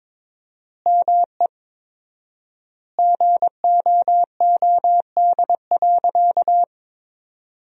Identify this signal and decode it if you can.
Morse code, heard as a keyed tone, about 21 wpm, 700 Hz. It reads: ME GOOD.
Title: 4x Word-Spacing - Sentences from Top 500 Words - 22wpm